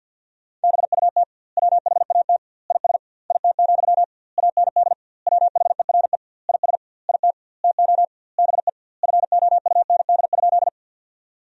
Morse code, heard as a keyed tone, about 50 wpm, 700 Hz. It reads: BUT WHAT IS IT, AND WHERE IS IT TO BE FOUND?
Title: 2x Word-Spacing - Sentences from Top 300 Words - 50wpm